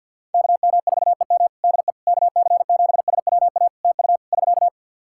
Morse code, heard as a keyed tone, about 49 wpm, 700 Hz. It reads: KM4EM DE KC7SWA TU <SK>
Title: Practice QSO - Exchange 4 - Ending - 50wpm